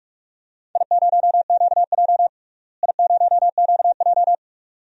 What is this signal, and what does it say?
Morse code, heard as a keyed tone, about 41 wpm, 700 Hz. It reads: I0QJ I0QJ
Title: DXCC-100 Call Signs - 4 Characters - Repeated 2x - 45wpm